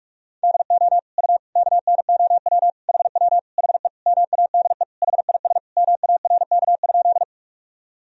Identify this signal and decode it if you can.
Morse code, heard as a keyed tone, about 43 wpm, 700 Hz. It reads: DO U KNOW HW HE MADE HIS MARK?
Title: Sentences from Top 300 Words and Abbreviations 45wpm